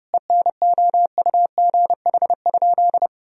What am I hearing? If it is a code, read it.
Morse code, heard as a keyed tone, about 30 wpm, 700 Hz. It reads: ENOUGH?
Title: Sentences from Top 1000 Words - Review Part 3 - 30wpm